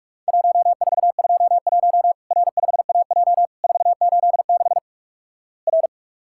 Morse code, heard as a keyed tone, about 48 wpm, 700 Hz. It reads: 1421 R5AJ 486 R
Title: Worked All Europe DX Contest - QTC Exchange - 45wpm